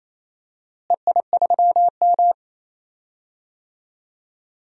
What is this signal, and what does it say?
Morse code, heard as a keyed tone, about 27 wpm, 700 Hz. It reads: EI3M
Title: DXCC-100 Call Signs - 4 Characters - Repeated 3x - 28wpm